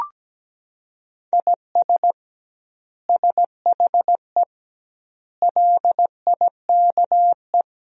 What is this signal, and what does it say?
Morse code, heard as a keyed tone, about 17 wpm, 700 Hz. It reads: IS SHE LIKE
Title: 2x Word-Spacing - Sentences from Top 100 Words - 17wpm